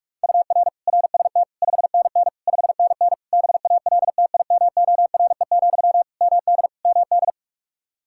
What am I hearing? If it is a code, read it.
Morse code, heard as a keyed tone, about 42 wpm, 700 Hz. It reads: UR RST 5NN 5NN BALTIMORE, MD MD
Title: Practice QSO - Exchange 1 - Beginning - 45wpm